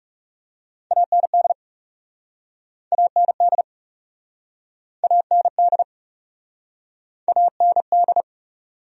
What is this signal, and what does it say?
Morse code, heard as a keyed tone, about 40 wpm, 700 Hz. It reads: AND AND AND AND